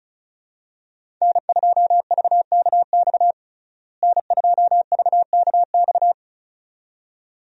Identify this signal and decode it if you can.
Morse code, heard as a keyed tone, about 34 wpm, 700 Hz. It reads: N2VKX N2VKX
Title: USA Call Signs - 5 Characters - Repeated 2x - 35wpm